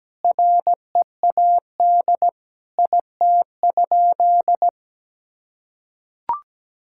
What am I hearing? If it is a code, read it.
Morse code, heard as a keyed tone, about 17 wpm, 700 Hz. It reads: READ IT?
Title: Sentences from Top 1600 Words 17wpm